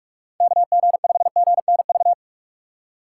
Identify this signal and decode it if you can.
Morse code, heard as a keyed tone, about 46 wpm, 700 Hz. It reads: KG5GNV
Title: USA Call Signs - 2x Character Spacing - 45wpm